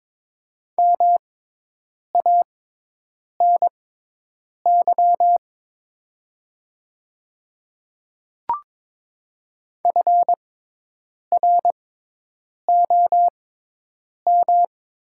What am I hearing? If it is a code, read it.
Morse code, heard as a keyed tone, about 22 wpm, spaced as for 8 wpm, 700 Hz. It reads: MANY FROM
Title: 6x Character Spacing - Top 200 Words - 22wpm